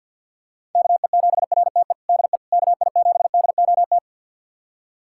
Morse code, heard as a keyed tone, about 52 wpm, 690 Hz. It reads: KE7RTE DE KI7DOT
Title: USA Callsign DE Callsign - 6 Characters - 50wpm